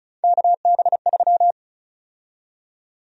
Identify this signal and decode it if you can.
Morse code, heard as a keyed tone, about 34 wpm, 700 Hz. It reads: KB3
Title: USA Call Sign Prefixes - 35wpm